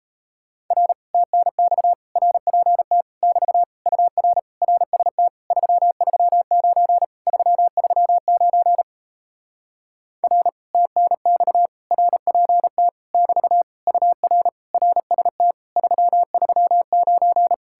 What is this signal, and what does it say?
Morse code, heard as a keyed tone, about 38 wpm, 700 Hz. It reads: R TNX RPT <BT> UR RST 339 339 R TNX RPT <BT> UR RST 339